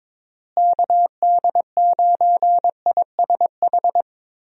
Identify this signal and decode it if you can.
Morse code, heard as a keyed tone, about 22 wpm, 700 Hz. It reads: KD9ISH
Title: USA Call Signs - 6 Characters - Repeated 3x - 22wpm